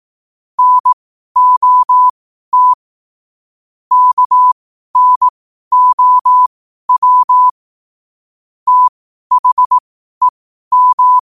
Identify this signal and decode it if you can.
Morse code, heard as a keyed tone, about 18 wpm, spaced as for 12 wpm, 1000 Hz. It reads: NOT KNOW THEM